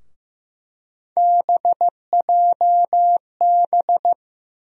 Morse code, heard as a keyed tone, about 15 wpm, 700 Hz. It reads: BJB
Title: USA Call Sign Postfixes - 3 Letters 15wpm